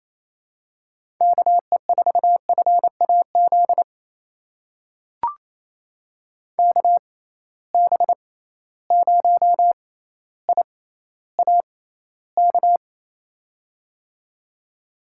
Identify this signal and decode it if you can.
Morse code, heard as a keyed tone, about 28 wpm, spaced as for 14 wpm, 700 Hz. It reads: KE4FAZ KB0IAK
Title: USA Call Signs - 6 Characters - 6x Character Spacing - 28wpm